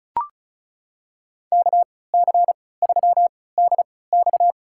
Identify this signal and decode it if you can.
Morse code, heard as a keyed tone, about 34 wpm, spaced as for 20 wpm, 700 Hz. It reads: KC3DX
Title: USA Call Signs - 5 Characters - 3x Character Spacing - 35wpm